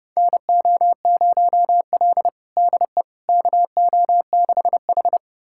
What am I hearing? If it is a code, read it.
Morse code, heard as a keyed tone, about 30 wpm, 700 Hz. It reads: NO0L DE KO6H
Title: USA Callsign DE Callsign - 4 Characters - 30wpm